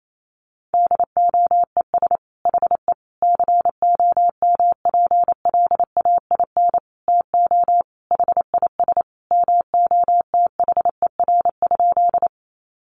Morse code, heard as a keyed tone, about 28 wpm, 700 Hz. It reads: DOES HE COMPLAIN TO HIS MOTHER?
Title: Sentences from Top 1300 Words 28wpm